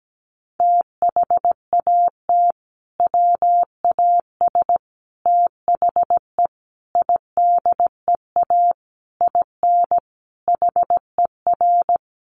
Morse code, heard as a keyed tone, about 17 wpm, 700 Hz. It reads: THAT WAS THE IDEA IN HER
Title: Sentences from Top 1100 Words 17wpm